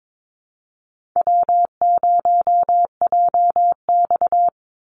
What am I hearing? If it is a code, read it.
Morse code, heard as a keyed tone, about 22 wpm, 700 Hz. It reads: W0JX
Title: Easy Call Signs Repeated 3x - 22wpm